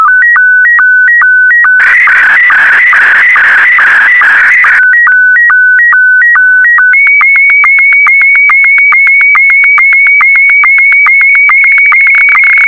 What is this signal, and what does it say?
SSTV (slow-scan television), Scottie 1.